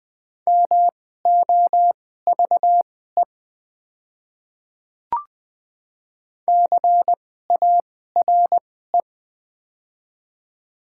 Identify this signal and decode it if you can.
Morse code, heard as a keyed tone, about 20 wpm, spaced as for 14 wpm, 700 Hz. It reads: MOVE CARE
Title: 2x Character Spacing - Top 300 Words - 20wpm